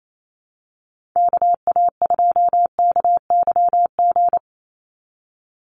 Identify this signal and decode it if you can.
Morse code, heard as a keyed tone, about 27 wpm, 700 Hz. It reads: KA2KYG
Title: USA Call Signs 28wpm